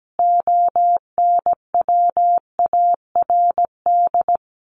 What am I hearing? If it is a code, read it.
Morse code, heard as a keyed tone, about 17 wpm, 700 Hz. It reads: ONWARD
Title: A Sound Of Thunder 17wpm